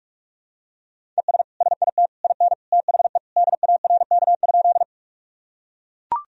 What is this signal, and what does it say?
Morse code, heard as a keyed tone, about 41 wpm, 700 Hz. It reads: ES SIT IN THE DARK?
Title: Sentences from Top 500 Words and Abbreviations 45wpm